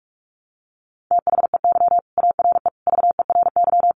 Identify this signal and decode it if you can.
Morse code, heard as a keyed tone, about 44 wpm, 700 Hz. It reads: THEY ARE VERY
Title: Sentences from Top 700 Words 45wpm